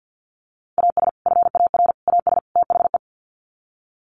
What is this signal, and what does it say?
Morse code, heard as a keyed tone, about 50 wpm, 700 Hz. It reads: AS FAR AS THE
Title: Sets of 4 Words 50wpm